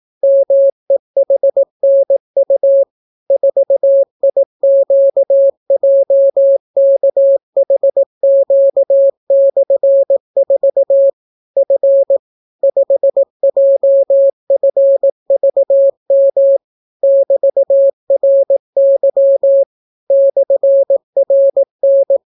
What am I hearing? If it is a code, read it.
Morse code, heard as a keyed tone, about 18 wpm, 550 Hz. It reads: MEHNU 4IQJKHQ/4 F 5JFVM <BT>RY /RN